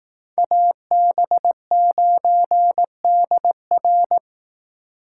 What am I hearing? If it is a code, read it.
Morse code, heard as a keyed tone, about 18 wpm, 700 Hz. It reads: AB9DR
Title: Speed Racing - USA 2x2 Call Signs 15wpm